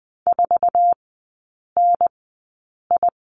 Morse code, heard as a keyed tone, about 20 wpm, spaced as for 8 wpm, 700 Hz. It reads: 4NI
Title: 074 Lesson 13e - Letter L - Three Characters 20wpm